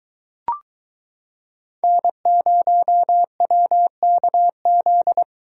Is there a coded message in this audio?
Yes: Morse code from 1.8 to 5.2 s, heard as a keyed tone, about 23 wpm, 700 Hz. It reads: N0WKZ